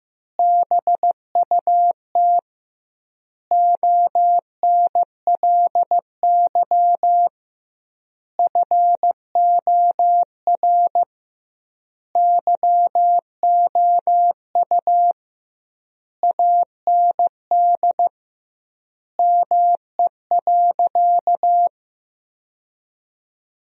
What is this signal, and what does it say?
Morse code, heard as a keyed tone, about 16 wpm, 700 Hz. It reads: BUT ONLY FOR YOU AND ME.
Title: 2x Word-Spacing - Sentences from Top 500 Words - 15wpm